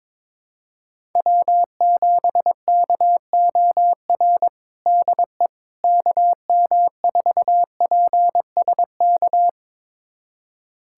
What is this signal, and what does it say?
Morse code, heard as a keyed tone, about 22 wpm, 700 Hz. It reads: W7KOR DE KM4PSK
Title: USA Callsign DE Callsign 22wpm